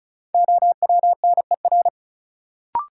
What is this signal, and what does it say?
Morse code, heard as a keyed tone, about 34 wpm, 700 Hz. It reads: OWNER
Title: Sentences from Top 1600 Words - New Word Drill - 35wpm